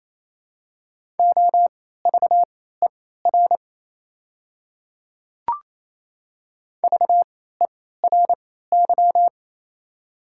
Morse code, heard as a keyed tone, about 28 wpm, spaced as for 16 wpm, 700 Hz. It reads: OVER VERY
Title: 3x Character Spacing - Top 200 Words - 28wpm